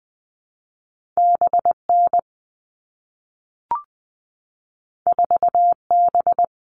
Morse code, heard as a keyed tone, about 20 wpm, 700 Hz. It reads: B N 4 B
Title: All Cut Numbers - 2-Digit 20wpm